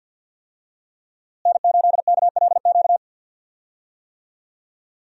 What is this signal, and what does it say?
Morse code, heard as a keyed tone, about 49 wpm, 700 Hz. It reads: N8KLQ
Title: USA Call Signs - 5 Characters - Repeated 3x - 50wpm